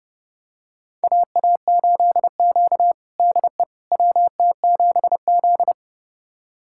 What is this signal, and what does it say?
Morse code, heard as a keyed tone, about 30 wpm, 700 Hz. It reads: AA8Q DE WT7Z
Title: USA Callsign DE Callsign - 4 Characters - 30wpm